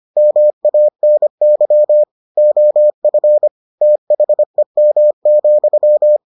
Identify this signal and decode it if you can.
Morse code, heard as a keyed tone, about 25 wpm, 600 Hz. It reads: MANY OF THEM,